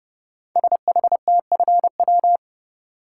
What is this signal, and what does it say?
Morse code, heard as a keyed tone, about 30 wpm, 700 Hz. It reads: SHTFW
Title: Code Groups - 5 Letters - 30wpm